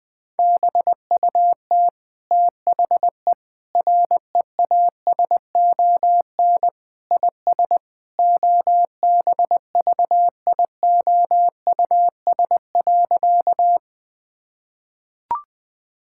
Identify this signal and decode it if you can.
Morse code, heard as a keyed tone, about 20 wpm, 700 Hz. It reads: BUT THE REASON IS OBVIOUS.
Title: Sentences from Top 1600 Words 20wpm